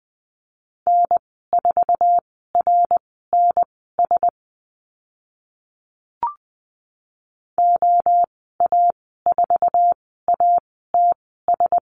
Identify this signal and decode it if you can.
Morse code, heard as a keyed tone, about 20 wpm, spaced as for 14 wpm, 700 Hz. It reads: N4RNS OA4ATS
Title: 062 Lesson 11g - Letter h - Call Signs 20wpm